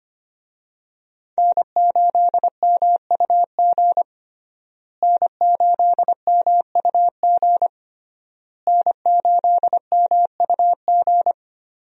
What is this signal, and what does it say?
Morse code, heard as a keyed tone, about 25 wpm, 700 Hz. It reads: N8MUG N8MUG N8MUG
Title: USA Call Signs - 5 Characters - Repeated 3x - 25wpm